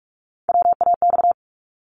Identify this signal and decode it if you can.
Morse code, heard as a keyed tone, about 46 wpm, 700 Hz. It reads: WAX
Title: Multi-letter Phonemes 45wpm